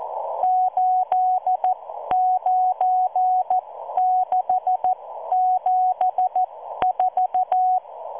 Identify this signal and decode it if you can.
Morse code, heard as a keyed tone, about 14 wpm, spaced as for 12 wpm, 740 Hz. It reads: 89674